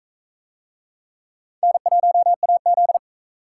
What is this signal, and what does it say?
Morse code, heard as a keyed tone, about 36 wpm, 700 Hz. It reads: N1AZ